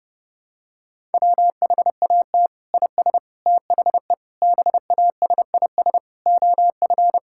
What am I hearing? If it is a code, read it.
Morse code, heard as a keyed tone, about 32 wpm, 700 Hz. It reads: WHAT IS THE BASIS OF
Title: Sentences from Top 1000 Words - Review Part 5 - 30wpm